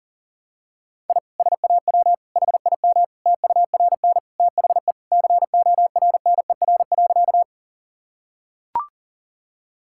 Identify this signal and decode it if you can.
Morse code, heard as a keyed tone, about 40 wpm, 700 Hz. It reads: I SAW HIM TURN THE CORNER.
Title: Sentences from Top 1600 Words 40wpm